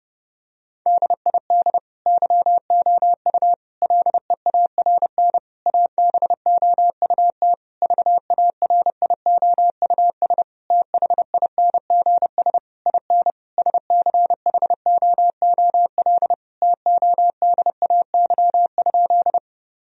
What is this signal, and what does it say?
Morse code, heard as a keyed tone, about 30 wpm, 700 Hz. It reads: DID YOU LEARN ABOUT VARIOUS THINGS IN SCHOOL TODAY?
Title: Sentences from Top 1000 Words - Review Part 4 - 30wpm